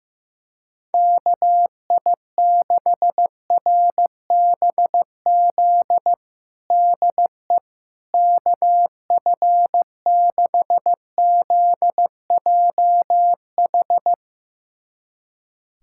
Morse code, heard as a keyed tone, about 15 wpm, 700 Hz. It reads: KI6RBZ DE KF6ZJH
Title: USA Callsign DE Callsign - 6 Characters - 15wpm